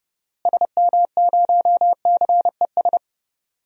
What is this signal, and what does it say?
Morse code, heard as a keyed tone, about 29 wpm, 700 Hz. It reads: SM0CES
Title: DXCC-100 Call Signs - 6 Characters - Repeated 2x - 30wpm